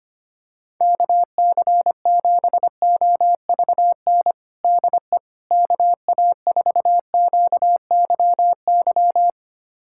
Morse code, heard as a keyed tone, about 25 wpm, 700 Hz. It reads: KC7OVN DE KA4QYY
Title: USA Callsign DE Callsign 25wpm